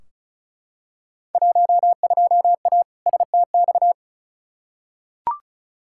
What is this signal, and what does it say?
Morse code, heard as a keyed tone, about 36 wpm, 700 Hz. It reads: 12A STX
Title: Field Day Exchanges Encore - 35wpm